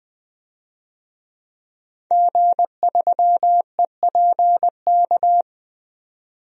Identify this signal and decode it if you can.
Morse code, heard as a keyed tone, about 20 wpm, 700 Hz. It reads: G3EPK
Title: DXCC-100 Call Signs - 5 Characters - 20wpm